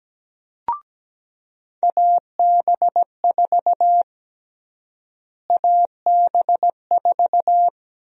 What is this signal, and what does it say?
Morse code, heard as a keyed tone, about 17 wpm, 700 Hz. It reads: AB4 AB4
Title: USA Call Sign Prefixes - Repeated 2x - 17wpm